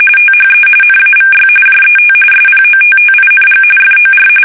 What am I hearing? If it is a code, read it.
RTTY, about 45.45 baud. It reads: THE NAVY JET SPOTTING FOR
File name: fsk850 / battleshipnj